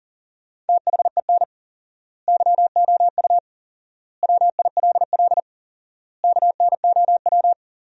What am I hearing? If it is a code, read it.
Morse code, heard as a keyed tone, about 37 wpm, 700 Hz. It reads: THEN YOU WILL KNOW
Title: 4x Word-Spacing - Sentences from Top 400 Words - 40wpm